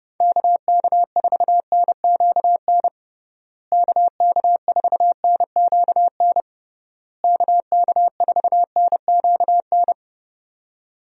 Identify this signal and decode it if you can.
Morse code, heard as a keyed tone, about 31 wpm, 700 Hz. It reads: KK4NQN KK4NQN KK4NQN
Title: USA Call Signs - 6 Characters - Repeated 3x - 30wpm